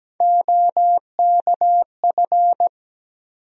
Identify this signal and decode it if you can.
Morse code, heard as a keyed tone, about 17 wpm, 700 Hz. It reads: OKF
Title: USA Call Sign Postfixes - 3 Letters 17wpm